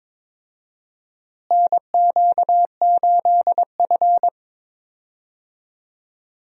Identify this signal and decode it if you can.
Morse code, heard as a keyed tone, about 22 wpm, 700 Hz. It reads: NQ8F
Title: USA Call Signs - 4 Characters - Repeated 3x - 22wpm